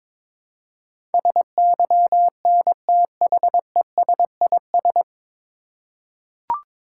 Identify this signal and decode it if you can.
Morse code, heard as a keyed tone, about 22 wpm, 700 Hz. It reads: SYNTHESIS
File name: 9 Letter Words - First 3 Letters Given - 22wpm